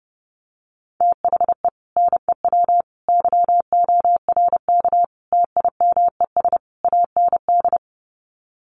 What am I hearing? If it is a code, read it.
Morse code, heard as a keyed tone, about 30 wpm, 700 Hz. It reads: THE NEW YORK TIMES AND
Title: Sets of 5 Words Encore 30wpm